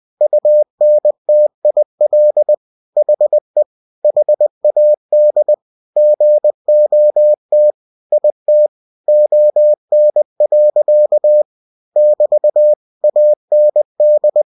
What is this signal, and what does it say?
Morse code, heard as a keyed tone, about 20 wpm, 600 Hz. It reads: UNTIL HE HAD GOT IT ON. <BT> AND
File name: cw-books / winnie-the-pooh / winnie-20wpm-page-0076